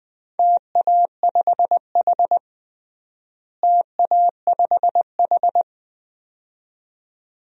Morse code, heard as a keyed tone, about 21 wpm, 700 Hz. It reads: TA5H TA5H
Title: DXCC-100 Call Signs - 4 Characters - Repeated 2x - 20wpm